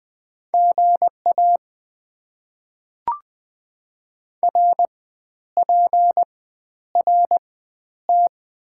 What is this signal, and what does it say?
Morse code, heard as a keyed tone, about 21 wpm, spaced as for 11 wpm, 690 Hz. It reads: GA RPRT
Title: Top 90 QSO Elements - 4x Character Spacing - 20wpm